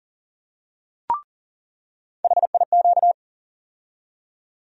Morse code, heard as a keyed tone, about 40 wpm, 700 Hz. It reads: HIQ
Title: USA Call Sign Postfixes - 3 Letters 40wpm